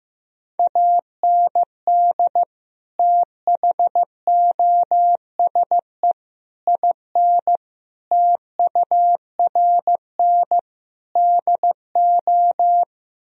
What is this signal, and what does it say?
Morse code, heard as a keyed tone, about 15 wpm, 700 Hz. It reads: AND THOSE IN TURN DO
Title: Sentences from Top 1000 Words - Review Part 1 - 15wpm